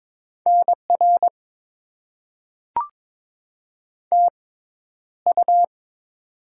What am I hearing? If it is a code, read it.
Morse code, heard as a keyed tone, about 22 wpm, spaced as for 11 wpm, 700 Hz. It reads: NR TU
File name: Top 90 QSO Elements - 6x Character Spacing - 22wpm